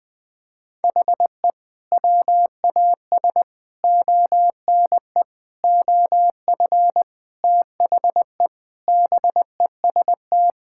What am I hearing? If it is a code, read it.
Morse code, heard as a keyed tone, about 20 wpm, 700 Hz. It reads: HE WAS ONE OF THE BEST
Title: Sentences with 5-Word Phrases - 20wpm